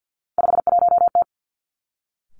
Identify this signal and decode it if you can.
Morse code, heard as a keyed tone, about 50 wpm, 700 Hz. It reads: 51T
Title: Three Challenging Letter-Number Combinations 50wpm